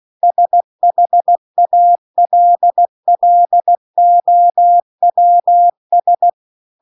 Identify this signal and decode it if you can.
Morse code, heard as a keyed tone, about 16 wpm, 700 Hz. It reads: SHALLOWS